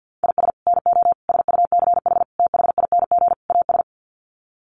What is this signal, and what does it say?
Morse code, heard as a keyed tone, about 50 wpm, 700 Hz. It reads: IS NO SUCH THING AS
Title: Sets of 5 Words Encore 50wpm